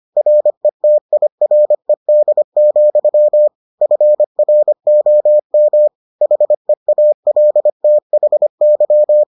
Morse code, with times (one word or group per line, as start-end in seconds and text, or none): 0.2-3.5 RETIRED,
3.8-5.9 FROM
6.2-9.2 HEALTHY